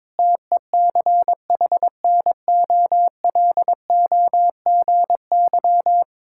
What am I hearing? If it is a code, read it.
Morse code, heard as a keyed tone, about 22 wpm, 700 Hz. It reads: TECHNOLOGY
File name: practice / 22wpm / Sentences with 3-Word Phrases - 22wpm